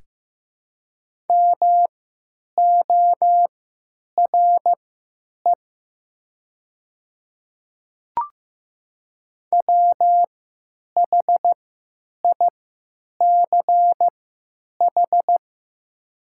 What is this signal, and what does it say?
Morse code, heard as a keyed tone, about 15 wpm, spaced as for 9 wpm, 700 Hz. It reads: MORE WHICH
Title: 3x Character Spacing - Top 200 Words - 15wpm